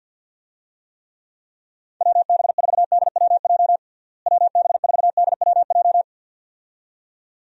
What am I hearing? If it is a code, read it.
Morse code, heard as a keyed tone, about 51 wpm, 700 Hz. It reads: WB4DWJ WB4DWJ
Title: USA Call Signs - 6 Characters - Repeated 2x - 50wpm